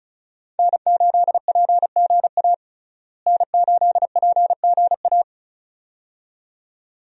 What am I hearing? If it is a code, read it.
Morse code, heard as a keyed tone, about 36 wpm, 700 Hz. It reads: N8PGA N8PGA
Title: USA Call Signs - 5 Characters - Repeated 2x - 35wpm